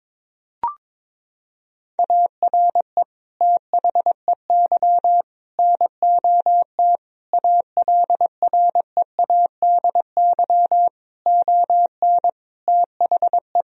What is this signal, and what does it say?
Morse code, heard as a keyed tone, about 22 wpm, 700 Hz. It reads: ARE THEY NOT ALREADY ON THE
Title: Sentences from Top 1000 Words - Review Part 1 - 22wpm